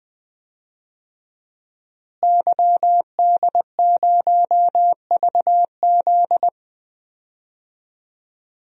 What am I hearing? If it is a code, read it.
Morse code, heard as a keyed tone, about 20 wpm, 700 Hz. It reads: YD0VZ
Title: DXCC-100 Call Signs - 5 Characters - Repeated 2x - 20wpm